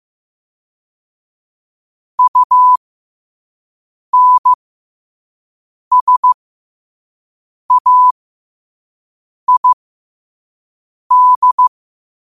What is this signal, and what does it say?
Morse code, heard as a keyed tone, about 15 wpm, spaced as for 5 wpm, 1000 Hz. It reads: UNSAID
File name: qotdfiles / 5wpm